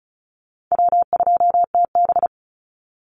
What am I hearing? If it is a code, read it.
Morse code, heard as a keyed tone, about 34 wpm, 700 Hz. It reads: W2TB
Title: Easy Call Signs Repeated 3x - 35wpm